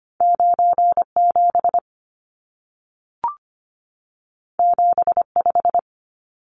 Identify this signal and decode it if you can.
Morse code, heard as a keyed tone, about 25 wpm, 700 Hz. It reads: 97 75
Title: Numbers - 2-Digit 25wpm